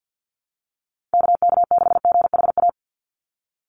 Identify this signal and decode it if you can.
Morse code, heard as a keyed tone, about 48 wpm, 700 Hz. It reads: KK6GHA
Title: Call Signs - 50wpm